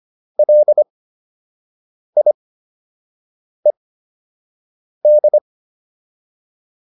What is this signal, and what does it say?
Morse code, heard as a keyed tone, about 25 wpm, spaced as for 6 wpm, 600 Hz. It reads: LIED